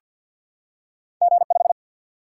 Morse code, heard as a keyed tone, about 48 wpm, 700 Hz. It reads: G5
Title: Single-Letter Single-Number 50wpm